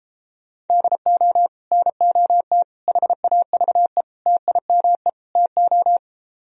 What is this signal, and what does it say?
Morse code, heard as a keyed tone, about 33 wpm, 700 Hz. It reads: DO NOT HAVE TIME TO